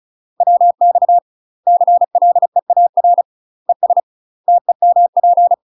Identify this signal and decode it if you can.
Morse code, heard as a keyed tone, about 35 wpm, 700 Hz. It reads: WX CLEAR ES TEMP
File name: UK MASTER QSOs TRAINING / 35WPM / UK MASTER QSOs 0018